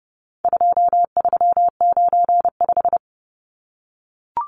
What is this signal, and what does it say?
Morse code, heard as a keyed tone, about 30 wpm, 700 Hz. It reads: 2395
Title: Numbers - 4-Digit 30wpm